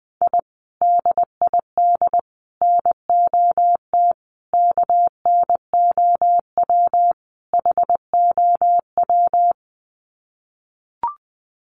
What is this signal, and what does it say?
Morse code, heard as a keyed tone, about 20 wpm, 700 Hz. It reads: I DID NOT KNOW HOW
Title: Sets of 5 Words Encore 20wpm